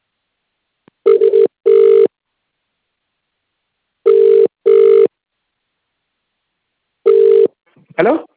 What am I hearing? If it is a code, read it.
Ringback tone (UK), 400 + 450 Hz, with speech.